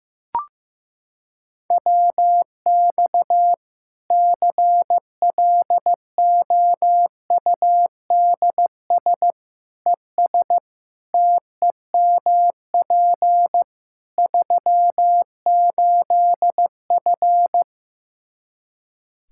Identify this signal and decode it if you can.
Morse code, heard as a keyed tone, about 16 wpm, 700 Hz. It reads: WX CLOUDS ES TEMP 38F
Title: Practice QSO - Exchange 2 - Info - 15wpm